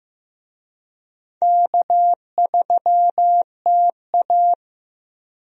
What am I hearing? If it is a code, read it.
Morse code, heard as a keyed tone, about 15 wpm, 700 Hz. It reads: K3TA
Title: USA Call Signs - 4 Characters - 15wpm